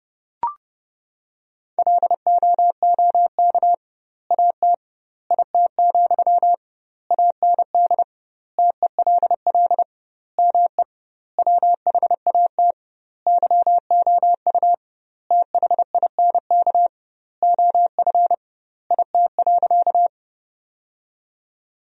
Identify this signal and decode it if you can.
Morse code, heard as a keyed tone, about 30 wpm, 700 Hz. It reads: LOOK AT IT, AND TELL ME WHAT YOU THINK OF IT.
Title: 2x Word-Spacing - Sentences from Top 200 Words - 30wpm